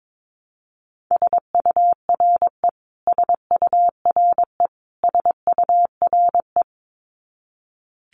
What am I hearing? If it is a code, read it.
Morse code, heard as a keyed tone, about 22 wpm, 700 Hz. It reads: SURE SURE SURE
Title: Top 500 Words Repeated 22wpm